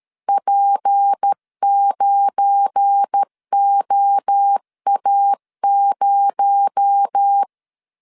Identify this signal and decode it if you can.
Morse code, heard as a keyed tone, about 13 wpm, 800 Hz. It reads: P9OA0
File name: LEZIONI / 7 GF9Z / 7 GF9 12wpm 2319